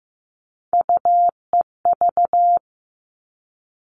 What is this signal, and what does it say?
Morse code, heard as a keyed tone, about 15 wpm, 700 Hz. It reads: UEV
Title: All Cut Numbers - 3-Digit 15wpm